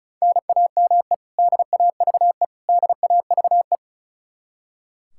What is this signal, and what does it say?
Morse code, heard as a keyed tone, about 36 wpm, 700 Hz. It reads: NAME DAVE DAVE
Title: Practice QSO - Exchange 1 - Beginning - 35wpm